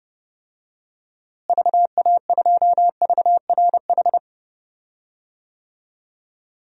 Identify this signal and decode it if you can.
Morse code, heard as a keyed tone, about 32 wpm, 700 Hz. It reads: VA2VRH
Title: DXCC-100 Call Signs - 6 Characters - Repeated 3x - 30wpm